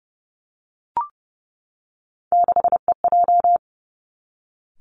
Morse code, heard as a keyed tone, about 30 wpm, 700 Hz. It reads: 6EJ
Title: Three Challenging Letter-Number Combinations 30wpm